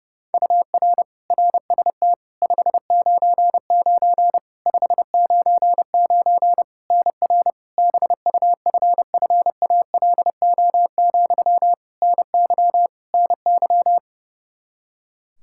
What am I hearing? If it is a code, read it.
Morse code, heard as a keyed tone, about 30 wpm, 700 Hz. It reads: UR RST 599 599 NR BUFFALO, NY NY
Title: K1USN SST QSO Practice 30wpm